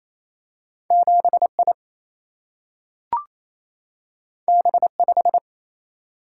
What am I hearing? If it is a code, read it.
Morse code, heard as a keyed tone, about 28 wpm, 700 Hz. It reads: 7I B5
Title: Single-Letter Single-Number 28wpm